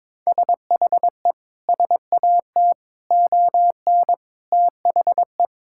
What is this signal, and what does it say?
Morse code, heard as a keyed tone, about 22 wpm, 700 Hz. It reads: SHE SAT ON THE
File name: practice / 22wpm / Sentences with 4-Word Phrases - 22wpm